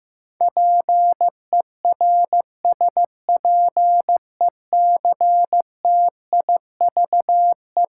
Morse code, heard as a keyed tone, about 15 wpm, 700 Hz. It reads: PERSPECTIVE
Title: Sentences from Top 1200 Words - New Word Drill Alt - 15wpm